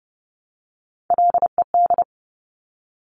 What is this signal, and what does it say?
Morse code, heard as a keyed tone, about 31 wpm, 700 Hz. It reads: LED
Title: Top Three Letter Combinations in Words 30wpm